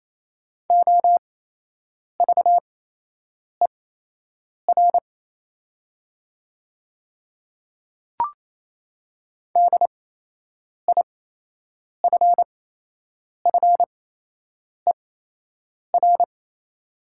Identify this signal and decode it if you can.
Morse code, heard as a keyed tone, about 28 wpm, spaced as for 8 wpm, 700 Hz. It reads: OVER DIFFER
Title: 8x Character Spacing - Top 300 Words - 28wpm